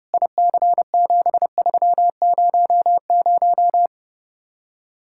Morse code, heard as a keyed tone, about 30 wpm, 700 Hz. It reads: IC7300